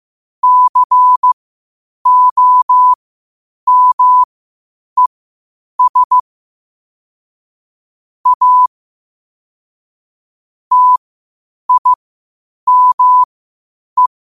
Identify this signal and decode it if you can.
Morse code, heard as a keyed tone, about 15 wpm, spaced as for 8 wpm, 1000 Hz. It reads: COMES A TIME
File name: qotdfiles / 8wpm